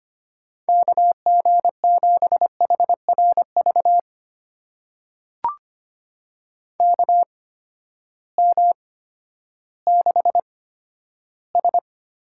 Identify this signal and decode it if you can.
Morse code, heard as a keyed tone, about 25 wpm, spaced as for 12 wpm, 700 Hz. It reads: KG7HRV KM6S
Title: USA Call Signs - 6 Characters - 8x Character Spacing - 25wpm